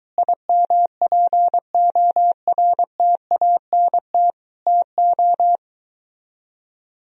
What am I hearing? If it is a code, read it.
Morse code, heard as a keyed tone, about 23 wpm, 700 Hz. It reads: IMPORTANT TO